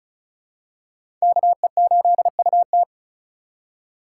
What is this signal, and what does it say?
Morse code, heard as a keyed tone, about 36 wpm, 700 Hz. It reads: KE8UT
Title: USA Call Signs - 4x Character Spacing - 35wpm